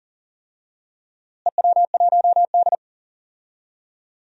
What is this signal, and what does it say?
Morse code, heard as a keyed tone, about 40 wpm, 700 Hz. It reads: EW1D
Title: DXCC-100 Call Signs - 4 Characters - Repeated 3x - 40wpm